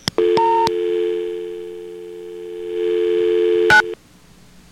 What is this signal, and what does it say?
Dial tone (North America), 350 + 440 Hz.